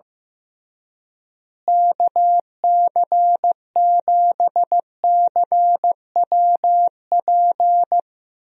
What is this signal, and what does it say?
Morse code, heard as a keyed tone, about 15 wpm, 700 Hz. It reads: KC7CWP
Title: USA Call Signs - 6 Characters - Repeated 3x - 15wpm